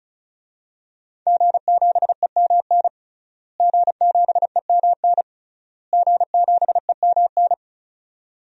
Morse code, heard as a keyed tone, about 34 wpm, 700 Hz. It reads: G7EMN G7EMN G7EMN
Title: DXCC-100 Call Signs - 5 Characters - Repeated 3x - 35wpm